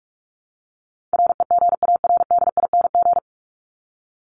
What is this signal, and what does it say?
Morse code, heard as a keyed tone, about 44 wpm, 700 Hz. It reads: REGARDING
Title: ING Suffix - Top 100 Words with Base Words 45wpm